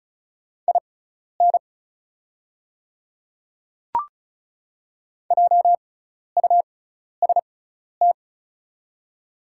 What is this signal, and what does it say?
Morse code, heard as a keyed tone, about 36 wpm, spaced as for 12 wpm, 700 Hz. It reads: IN JUST
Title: 6x Character Spacing - Top 200 Words - 35wpm